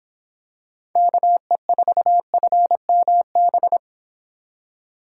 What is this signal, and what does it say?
Morse code, heard as a keyed tone, about 26 wpm, 700 Hz. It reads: KE4FMB